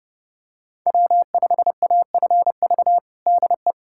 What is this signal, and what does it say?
Morse code, heard as a keyed tone, about 30 wpm, 700 Hz. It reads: W5AFV DE
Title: USA Callsign DE Callsign 30wpm